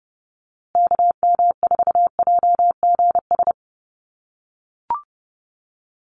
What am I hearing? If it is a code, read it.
Morse code, heard as a keyed tone, about 30 wpm, 700 Hz. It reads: KM4JGS
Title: USA Call Signs Repeated 3x 30wpm